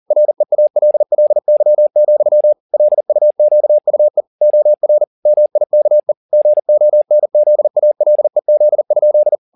Morse code, heard as a keyed tone, about 40 wpm, 600 Hz. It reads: REALLY, LUQUE OR MIKE GONZALEZ?